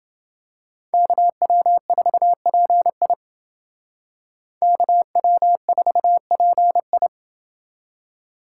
Morse code, heard as a keyed tone, about 28 wpm, 700 Hz. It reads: KW4PI KW4PI